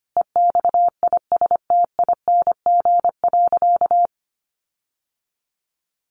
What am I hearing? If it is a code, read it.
Morse code, heard as a keyed tone, about 25 wpm, 700 Hz. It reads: EXISTING.
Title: Sentences from Top 1500 Words 25wpm